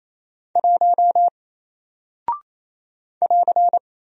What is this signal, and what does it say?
Morse code, heard as a keyed tone, about 28 wpm, 700 Hz. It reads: 1 <AR>
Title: Single Character (Letter:Number:Punctuation:Prosign) Alt 28wpm